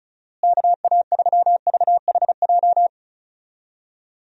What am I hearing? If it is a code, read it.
Morse code, heard as a keyed tone, about 36 wpm, 700 Hz. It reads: KA3VHJ
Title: USA Call Signs - 6 Characters - Repeated 2x - 35wpm